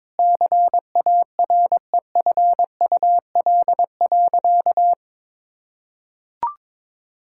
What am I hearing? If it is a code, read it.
Morse code, heard as a keyed tone, about 22 wpm, 700 Hz. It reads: CAREFUL.
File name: practice / 22wpm / Sentences with 3-Word Phrases - 22wpm